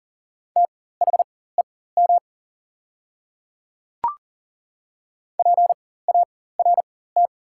Morse code, heard as a keyed tone, about 40 wpm, spaced as for 19 wpm, 690 Hz. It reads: THEM PART
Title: 4x Character Spacing - Top 200 Words - 40wpm